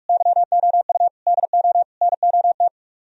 Morse code, heard as a keyed tone, about 59 wpm, spaced as for 52 wpm, 700 Hz. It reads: YOU DO NOT
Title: Speed-Racing - Sets of 4 Words 45wpm